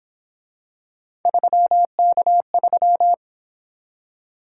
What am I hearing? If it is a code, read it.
Morse code, heard as a keyed tone, about 28 wpm, 700 Hz. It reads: 3K3
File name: Speed-Racing - Three Letter-Number Combinations - 22wpm